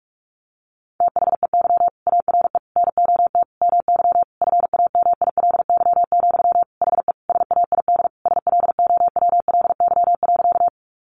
Morse code, heard as a keyed tone, about 44 wpm, 700 Hz. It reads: THEY ARE NOT MY FAMILY, HE SAID SLOWLY.
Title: Sentences from Top 1100 Words 45wpm